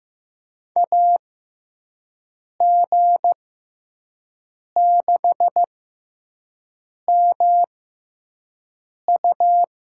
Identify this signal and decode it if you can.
Morse code, heard as a keyed tone, about 15 wpm, spaced as for 5 wpm, 700 Hz. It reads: AG6MU
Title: USA Call Signs - 5 Characters - 6x Character Spacing - 15wpm